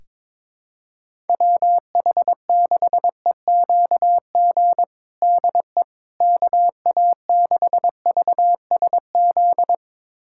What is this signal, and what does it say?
Morse code, heard as a keyed tone, about 21 wpm, 700 Hz. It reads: WH6EQG DE KA6VSZ
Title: USA Callsign DE Callsign - 6 Characters - 22wpm